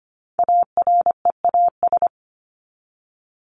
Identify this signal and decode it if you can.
Morse code, heard as a keyed tone, about 25 wpm, 700 Hz. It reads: AREAS
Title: ES:S Suffix - Top 100 Words with Base Words 25wpm